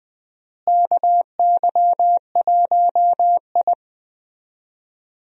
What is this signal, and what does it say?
Morse code, heard as a keyed tone, about 20 wpm, 700 Hz. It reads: KY1I
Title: USA Call Signs - 4 Characters - Repeated 2x - 20wpm